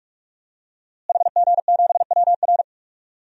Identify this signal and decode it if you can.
Morse code, heard as a keyed tone, about 47 wpm, 700 Hz. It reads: HG7WR
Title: DXCC-100 Call Signs - 5 Characters - Repeated 2x - 45wpm